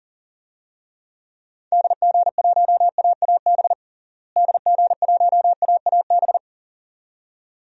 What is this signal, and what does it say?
Morse code, heard as a keyed tone, about 40 wpm, 700 Hz. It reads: DG1AAB DG1AAB
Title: DXCC-100 Call Signs - 6 Characters - Repeated 2x - 40wpm